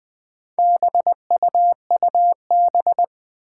Morse code, heard as a keyed tone, about 20 wpm, 700 Hz. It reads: BUUB
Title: Four Challenging Letter-Number Combinations 20wpm